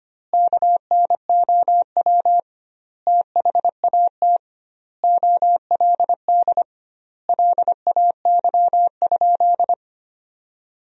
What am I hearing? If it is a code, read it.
Morse code, heard as a keyed tone, about 25 wpm, 700 Hz. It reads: KNOW THAT OLD LAY?
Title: 2x Word-Spacing - Sentences from Top 400 Words - 25wpm